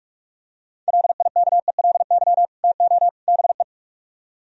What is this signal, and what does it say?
Morse code, heard as a keyed tone, about 49 wpm, 700 Hz. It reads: LIKELY TO BE